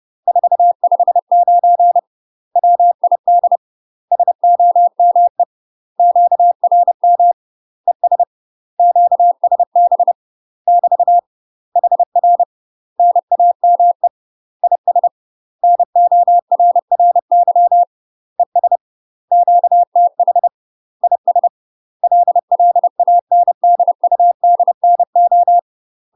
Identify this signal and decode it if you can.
Morse code, heard as a keyed tone, about 30 wpm, 700 Hz. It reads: 459 WID SOME QRM ES QSB <BT> HR NAME IS NORRY ES QTH IS LLANDUDNO
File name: UK MASTER QSOs TRAINING / 30WPM / UK MASTER QSOs 0005